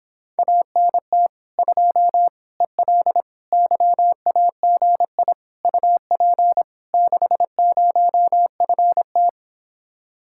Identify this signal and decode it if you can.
Morse code, heard as a keyed tone, about 26 wpm, 700 Hz. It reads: ANT 2 EL YAGI UP 60FT